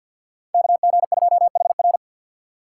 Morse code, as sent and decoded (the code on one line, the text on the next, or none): -.- --. ..--- .... .-.
KG2HR